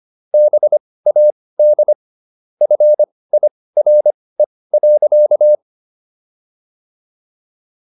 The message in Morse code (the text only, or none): BAD FIRE.